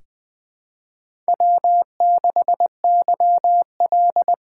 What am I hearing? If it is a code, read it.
Morse code, heard as a keyed tone, about 20 wpm, 700 Hz. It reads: W6YL
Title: USA Call Signs - 4 Characters - 3x Character Spacing - 20wpm